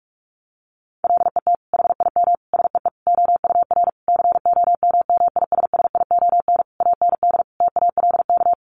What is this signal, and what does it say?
Morse code, heard as a keyed tone, about 46 wpm, 700 Hz. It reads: LET HIM SEE OUR COMMISSION AND TALK
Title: Sentences from Top 1400 Words 45wpm